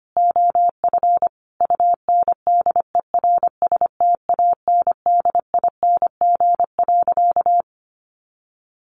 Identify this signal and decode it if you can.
Morse code, heard as a keyed tone, about 25 wpm, 700 Hz. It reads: OF UNDERSTANDING.
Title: Sentences from Top 1400 Words 25wpm